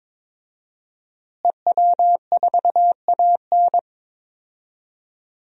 Morse code, heard as a keyed tone, about 21 wpm, 700 Hz. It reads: EW4AN
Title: DXCC-100 Call Signs - 5 Characters - Repeated 3x - 22wpm